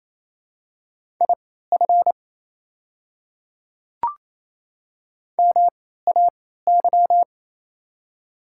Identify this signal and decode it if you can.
Morse code, heard as a keyed tone, about 28 wpm, spaced as for 16 wpm, 700 Hz. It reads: IF MAY
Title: 3x Character Spacing - Top 300 Words - 28wpm